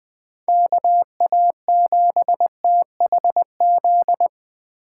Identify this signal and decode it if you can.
Morse code, heard as a keyed tone, about 24 wpm, 700 Hz. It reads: KA7THZ
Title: Speed Racing - USA Call Signs 20wpm